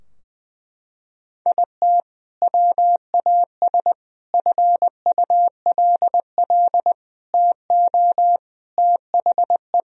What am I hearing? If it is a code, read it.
Morse code, heard as a keyed tone, about 21 wpm, 700 Hz. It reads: IT WAS FULL TO THE
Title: Sentences from Top 1000 Words - Review Part 3 - 20wpm